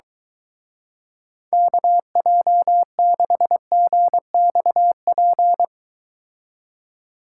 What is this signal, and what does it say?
Morse code, heard as a keyed tone, about 23 wpm, 700 Hz. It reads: KJ6GXP